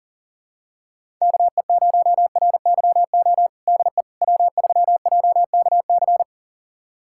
Morse code, heard as a keyed tone, about 40 wpm, 700 Hz. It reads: KE0RYO DE W3JKC
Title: USA Callsign DE Callsign 40wpm